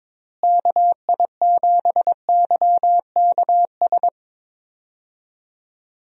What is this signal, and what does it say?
Morse code, heard as a keyed tone, about 22 wpm, 700 Hz. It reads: KI7YKS
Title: USA Call Signs - 6 Characters - Repeated 3x - 22wpm